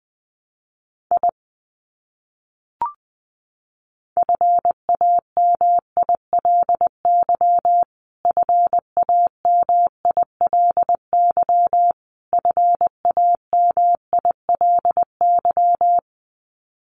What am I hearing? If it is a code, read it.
Morse code, heard as a keyed tone, about 20 wpm, 700 Hz. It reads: I FAMILY FAMILY FAMILY
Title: Top 500 Words Repeated 20wpm